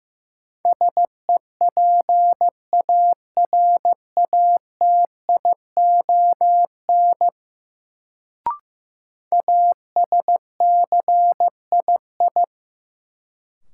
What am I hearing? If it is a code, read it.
Morse code, heard as a keyed tone, about 16 wpm, 700 Hz. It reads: SEPARATION ASCII
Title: Sets of 5 Words - New Word Drill - 15wpm